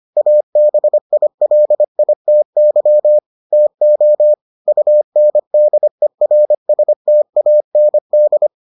Morse code, heard as a keyed tone, about 25 wpm, 600 Hz. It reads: ABILITY TO UNDERSTAND